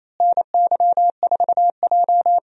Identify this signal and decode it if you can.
Morse code, heard as a keyed tone, about 28 wpm, 700 Hz. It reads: NY4J
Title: USA Call Signs - 4 Characters - Repeated 3x - 28wpm